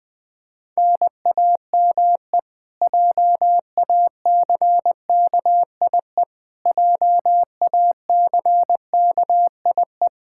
Morse code, heard as a keyed tone, about 20 wpm, 700 Hz. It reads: NAME JACKIE JACKIE
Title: Practice QSO - Exchange 1 - Beginning - 20wpm